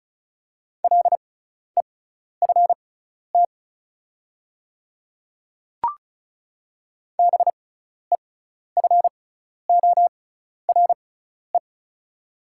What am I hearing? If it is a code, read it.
Morse code, heard as a keyed tone, about 36 wpm, spaced as for 12 wpm, 690 Hz. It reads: LEFT BEFORE
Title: 6x Character Spacing - Top 300 Words - 35wpm